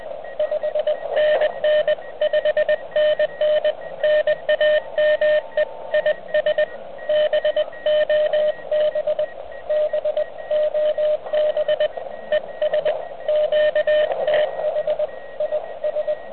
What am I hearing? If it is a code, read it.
Morse code, heard as a keyed tone, about 19 wpm, 610 Hz, with radio noise. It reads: 5NN 5NN NAME IS BOB BOB ES QTH IS